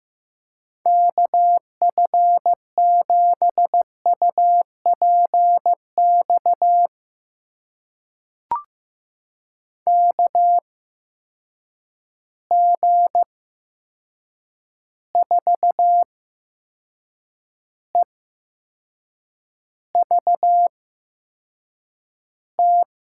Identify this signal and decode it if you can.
Morse code, heard as a keyed tone, about 15 wpm, spaced as for 6 wpm, 700 Hz. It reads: KF7UPX KG4EVT
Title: USA Call Signs - 6 Characters - 8x Character Spacing - 15wpm